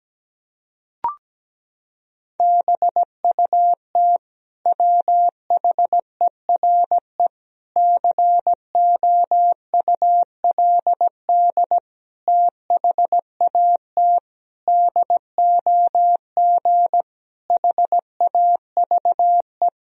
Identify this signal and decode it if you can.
Morse code, heard as a keyed tone, about 17 wpm, 700 Hz. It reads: BUT WHERE COULD THAT DOG HAVE
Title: Sentences from Top 1000 Words - Review Part 2 - 17wpm